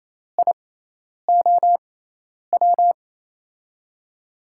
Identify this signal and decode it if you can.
Morse code, heard as a keyed tone, about 28 wpm, spaced as for 10 wpm, 700 Hz. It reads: IOW
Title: USA Call Signs - 5 Characters - 6x Character Spacing - 28wpm